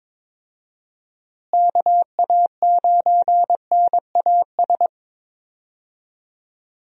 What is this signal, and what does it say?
Morse code, heard as a keyed tone, about 22 wpm, 700 Hz. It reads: KA9NAS
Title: USA Call Signs - 6 Characters - Repeated 2x - 22wpm